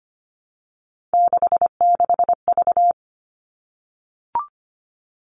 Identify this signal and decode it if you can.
Morse code, heard as a keyed tone, about 25 wpm, 700 Hz. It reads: 66V
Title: All Cut Numbers - 3-Digit 25wpm